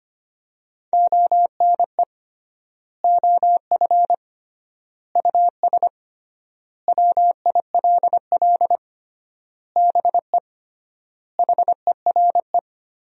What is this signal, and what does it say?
Morse code, heard as a keyed tone, about 25 wpm, 700 Hz. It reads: ONE OF US WILL BE HERE
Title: 3x Word-Spacing - Sentences from Top 300 Words - 25wpm